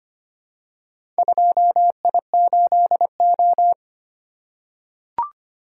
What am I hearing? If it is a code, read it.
Morse code, heard as a keyed tone, about 25 wpm, 700 Hz. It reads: 2I8O
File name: practice / 25wpm / Four Letter-Number Combinations - 25wpm